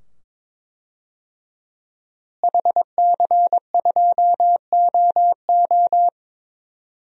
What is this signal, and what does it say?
Morse code, heard as a keyed tone, about 22 wpm, 700 Hz. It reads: HC2OO
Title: DXCC-100 Call Signs - 5 Characters - Repeated 2x - 22wpm